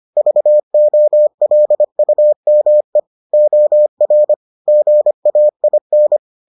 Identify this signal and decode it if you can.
Morse code, heard as a keyed tone, about 25 wpm, 600 Hz. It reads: VOLUME OR GAIN